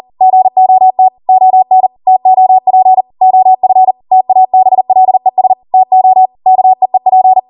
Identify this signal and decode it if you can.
Morse code, heard as a keyed tone, about 40 wpm, 750 Hz. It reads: GOT ON TOP OF TABLES TO KEEP